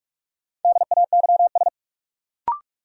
Morse code, heard as a keyed tone, about 45 wpm, 700 Hz. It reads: DAYS
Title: Sets of 5 Words Encore - New Word Drill - 45wpm